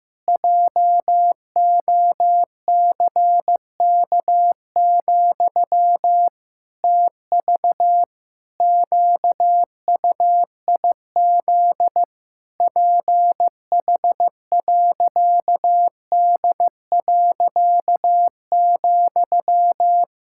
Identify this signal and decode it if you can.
Morse code, heard as a keyed tone, about 16 wpm, 700 Hz. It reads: JOCK, TV QUIZ PH.D.,
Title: Pangram Challenge v2 - 15wpm